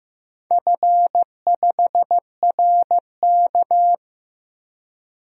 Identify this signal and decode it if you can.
Morse code, heard as a keyed tone, about 15 wpm, 700 Hz. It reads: F5RK
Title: DXCC-100 Call Signs - 4 Characters - 15wpm